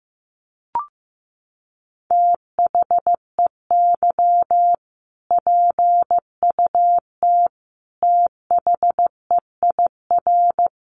Morse code, heard as a keyed tone, about 15 wpm, 700 Hz. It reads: THEY PUT THEIR
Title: Sentences from Top 1200 Words 15wpm